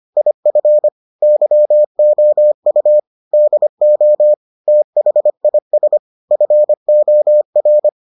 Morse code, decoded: IF YOU DO THIS FOR

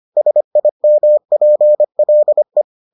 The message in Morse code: SIMPLE

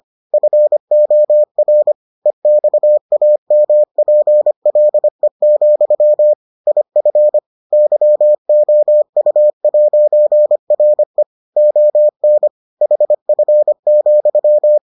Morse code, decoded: FOR EXAMPLE, IF YOU'RE ON HF,

